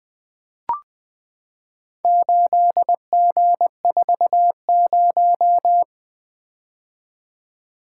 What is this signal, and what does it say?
Morse code, heard as a keyed tone, about 20 wpm, 700 Hz. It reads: 8G40
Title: Four Letter-Number Combinations - 20wpm